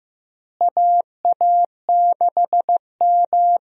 Morse code, heard as a keyed tone, about 16 wpm, 700 Hz. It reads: AA6M
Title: USA Call Signs - 4 Characters - Repeated 3x - 15wpm